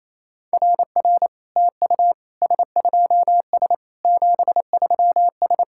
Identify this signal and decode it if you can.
Morse code, heard as a keyed tone, about 28 wpm, 700 Hz. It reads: RR TU S2S 73S
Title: SOTA QSO Practice 28wpm